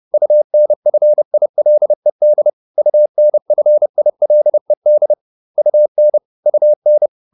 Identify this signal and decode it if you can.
Morse code, heard as a keyed tone, about 30 wpm, 600 Hz. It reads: UNFILED UNFILED UN UN